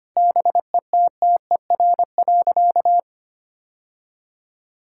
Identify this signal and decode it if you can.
Morse code, heard as a keyed tone, about 25 wpm, 700 Hz. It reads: BETTER.
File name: practice / 25wpm / Difficult Sentences with Hints - 25wpm